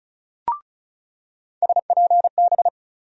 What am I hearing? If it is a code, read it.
Morse code, heard as a keyed tone, about 36 wpm, 700 Hz. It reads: SPB
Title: USA Call Sign Postfixes - 3 Letters 35wpm